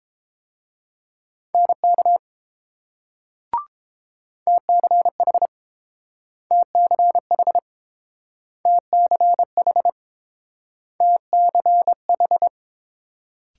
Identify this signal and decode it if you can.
Morse code, heard as a keyed tone, about 30 wpm, 700 Hz. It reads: NK TCH TCH TCH TCH